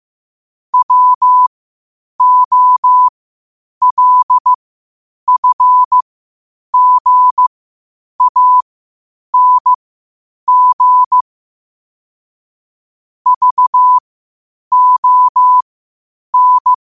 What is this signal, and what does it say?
Morse code, heard as a keyed tone, about 15 wpm, spaced as for 8 wpm, 1000 Hz. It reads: WOLFGANG VON
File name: qotdfiles / 8wpm